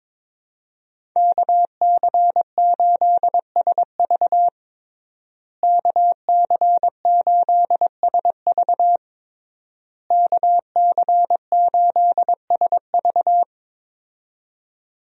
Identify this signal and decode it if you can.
Morse code, heard as a keyed tone, about 22 wpm, 700 Hz. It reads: KC8SV KC8SV KC8SV
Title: USA Call Signs - 5 Characters - Repeated 3x - 22wpm